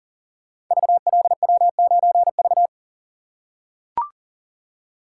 Morse code, heard as a keyed tone, about 40 wpm, 700 Hz. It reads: VLW9V
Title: Code Groups - 5 Letter:Numbers - 40wpm